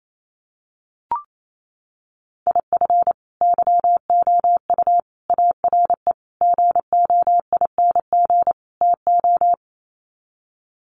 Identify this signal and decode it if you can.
Morse code, heard as a keyed tone, about 28 wpm, 700 Hz. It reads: IF YOU ARE GOING TO
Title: Sets of 5 Words Encore2 28wpm